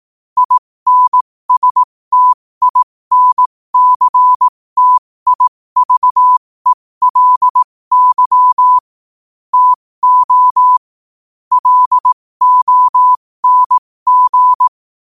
Morse code, heard as a keyed tone, about 18 wpm, spaced as for 15 wpm, 1000 Hz. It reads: INSTINCTIVELY TO LONG
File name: 15wpm